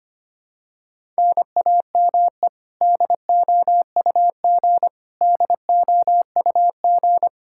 Morse code, heard as a keyed tone, about 25 wpm, 700 Hz. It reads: NAME DOUG DOUG
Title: Practice QSO - Exchange 1 - Beginning - 25wpm